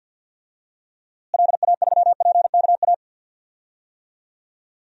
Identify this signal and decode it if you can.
Morse code, heard as a keyed tone, about 48 wpm, 700 Hz. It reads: LA3PKA
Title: DXCC-100 Call Signs - 6 Characters - Repeated 2x - 50wpm